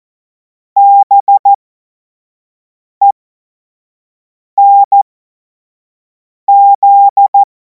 Morse code, heard as a keyed tone, about 14 wpm, spaced as for 5 wpm, 800 Hz. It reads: BENZ
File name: Morsekurs / Witze / Witz 07. Juli 2026 25 F70